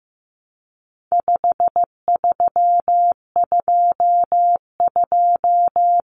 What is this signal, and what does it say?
Morse code, heard as a keyed tone, about 15 wpm, 700 Hz. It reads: 5322
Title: Common Cut Numbers - 4-Digit 15wpm